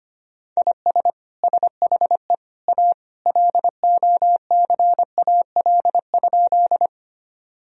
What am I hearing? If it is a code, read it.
Morse code, heard as a keyed tone, about 25 wpm, 700 Hz. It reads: IS SHE A LOCAL?
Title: Sentences from Top 1000 Words - Review Part 3 - 25wpm